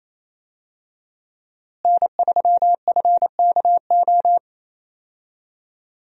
Morse code, heard as a keyed tone, about 28 wpm, 700 Hz. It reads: N3FKO
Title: USA Call Signs - 5 Characters - Repeated 2x - 28wpm